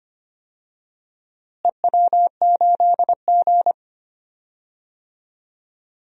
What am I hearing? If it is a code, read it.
Morse code, heard as a keyed tone, about 25 wpm, 700 Hz. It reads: EW8G
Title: DXCC-100 Call Signs - 4 Characters - Repeated 3x - 25wpm